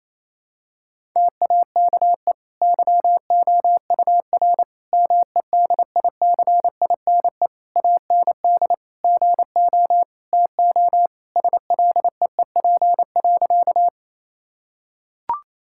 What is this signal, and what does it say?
Morse code, heard as a keyed tone, about 28 wpm, 700 Hz. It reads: TAKE YOUR MEDICINE AND GO TO SLEEP.
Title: Sentences from Top 1000 Words - Review Part 6 - 28wpm